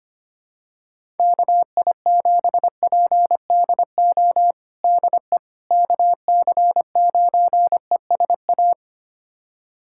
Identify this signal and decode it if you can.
Morse code, heard as a keyed tone, about 25 wpm, 700 Hz. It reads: KI7PDO DE KC9ESA
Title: USA Callsign DE Callsign - 6 Characters - 25wpm